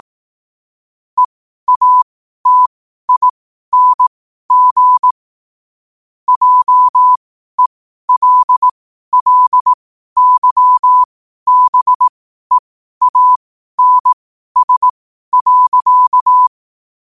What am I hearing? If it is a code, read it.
Morse code, heard as a keyed tone, about 18 wpm, spaced as for 12 wpm, 1000 Hz. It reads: EATING JELLYBEANS.
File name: qotdfiles / 12wpm